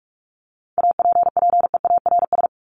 Morse code, heard as a keyed tone, about 44 wpm, 700 Hz. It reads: APPEARS
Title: ES:S Suffix - Top 100 Words with Base Words 45wpm